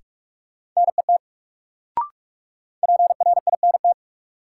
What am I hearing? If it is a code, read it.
Morse code, heard as a keyed tone, about 46 wpm, 700 Hz. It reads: NET PRINT
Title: Sentences from Top 1400 Words - New Word Drill - 45wpm